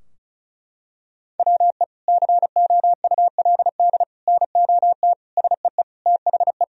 Morse code, heard as a keyed tone, about 36 wpm, 700 Hz. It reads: WE COULD NOT SEE THE
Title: Sentences from Top 1000 Words - Review Part 2 - 35wpm